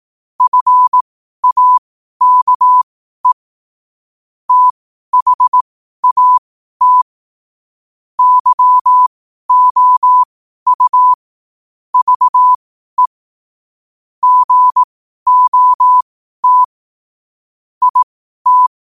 Morse code, heard as a keyed tone, about 18 wpm, spaced as for 12 wpm, 1000 Hz. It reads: FAKE THAT YOU VE GOT IT
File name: qotdfiles / 12wpm